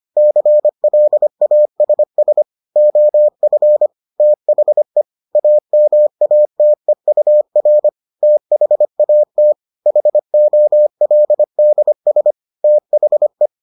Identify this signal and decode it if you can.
Morse code, heard as a keyed tone, about 25 wpm, 600 Hz. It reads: CLASS OF THE AMATEUR THAT HOLDS THE